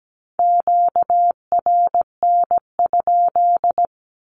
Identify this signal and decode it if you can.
Morse code, heard as a keyed tone, about 17 wpm, 700 Hz. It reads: QRN?
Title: Q-Codes 17wpm